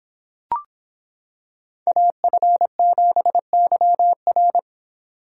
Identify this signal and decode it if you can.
Morse code, heard as a keyed tone, about 26 wpm, 700 Hz. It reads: AF7YR